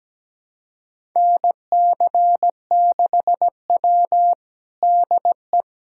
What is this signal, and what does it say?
Morse code, heard as a keyed tone, about 17 wpm, 700 Hz. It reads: NC6W DE
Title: USA Callsign DE Callsign - 4 Characters - 17wpm